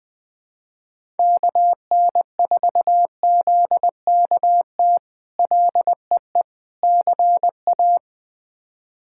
Morse code, heard as a keyed tone, about 21 wpm, 700 Hz. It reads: KN4ZKT LEE CA
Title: K1USN SST QSO Practice - 20wpm